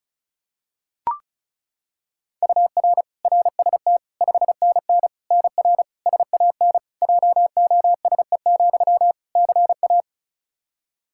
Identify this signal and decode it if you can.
Morse code, heard as a keyed tone, about 36 wpm, 700 Hz. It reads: UR RST 5NN NR SAN JOSE, CA
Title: Practice QSO - Exchange 1 - Beginning - 35wpm